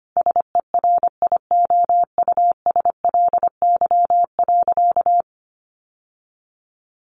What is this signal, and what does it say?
Morse code, heard as a keyed tone, about 25 wpm, 700 Hz. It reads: SERIOUSLY.
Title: Sentences from Top 1200 Words 25wpm